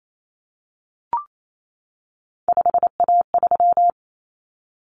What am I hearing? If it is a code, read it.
Morse code, heard as a keyed tone, about 28 wpm, 700 Hz. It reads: 5A3
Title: Common Cut Numbers - 3-Digit 28wpm